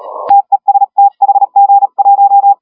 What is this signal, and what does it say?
Morse code, heard as a keyed tone, about 37 wpm, 800 Hz. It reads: TEST HG1
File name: HG1R CW SSB MIX